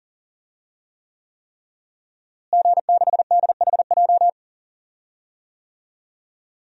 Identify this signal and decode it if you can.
Morse code, heard as a keyed tone, about 40 wpm, 700 Hz. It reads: G6DHJ
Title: DXCC-100 Call Signs - 5 Characters - Repeated 3x - 40wpm